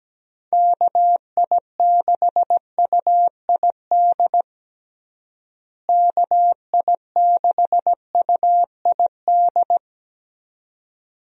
Morse code, heard as a keyed tone, about 17 wpm, 700 Hz. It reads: KI6UID KI6UID
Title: USA Call Signs - 6 Characters - Repeated 2x - 17wpm